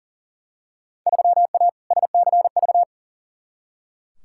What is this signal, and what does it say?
Morse code, heard as a keyed tone, about 38 wpm, 700 Hz. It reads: 3A SCV
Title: Field Day Exchanges - 40wpm